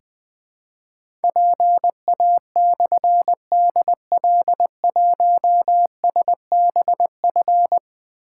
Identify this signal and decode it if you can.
Morse code, heard as a keyed tone, about 20 wpm, 700 Hz. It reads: PA/DL1SBF
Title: Difficult Call Signs 20wpm